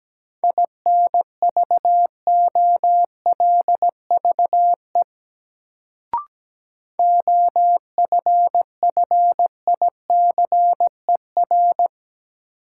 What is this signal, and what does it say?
Morse code, heard as a keyed tone, about 17 wpm, 700 Hz. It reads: INVOLVE OFFICER
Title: Sentences from Top 1200 Words - New Word Drill - 17wpm